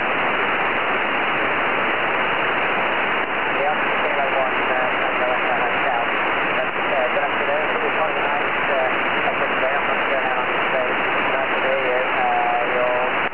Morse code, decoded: none

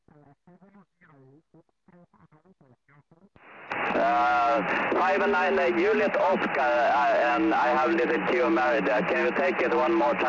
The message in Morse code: none

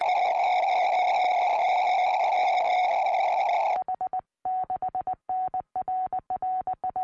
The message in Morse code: none

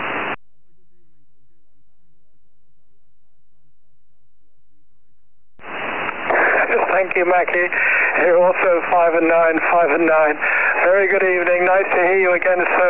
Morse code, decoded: none

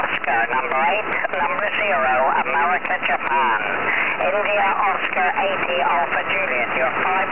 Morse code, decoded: none

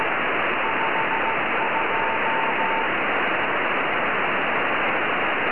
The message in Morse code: none

